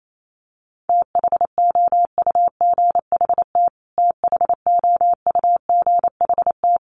Morse code, heard as a keyed tone, about 28 wpm, 700 Hz. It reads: THOUGHT THOUGHT
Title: Top 300 Words Repeated 28wpm